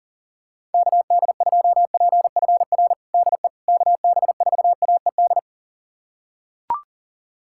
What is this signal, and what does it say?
Morse code, heard as a keyed tone, about 40 wpm, 700 Hz. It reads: KD2PFR DE KB4AED
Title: USA Callsign DE Callsign - 6 Characters - 40wpm